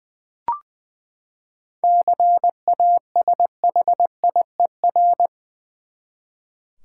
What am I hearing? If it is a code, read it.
Morse code, heard as a keyed tone, about 21 wpm, 700 Hz. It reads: CASHIER
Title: Practice QSO - Exchange 3 - More Info - 20wpm